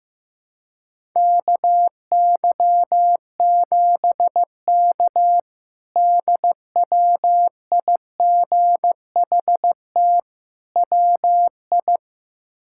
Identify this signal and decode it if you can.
Morse code, heard as a keyed tone, about 16 wpm, 700 Hz. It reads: KY7K DWIGHT WI
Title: K1USN SST QSO Practice - 15wpm